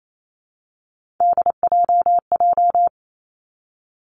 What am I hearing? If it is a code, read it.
Morse code, heard as a keyed tone, about 28 wpm, 690 Hz. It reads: DJJ
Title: Three Challenging Letter-Number Combinations 28wpm